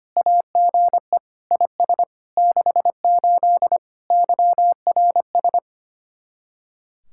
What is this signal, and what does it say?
Morse code, heard as a keyed tone, about 31 wpm, 700 Hz. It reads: AGE IS 68 YRS
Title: Speed-Racing Common QSO Phrases Encore - 25wpm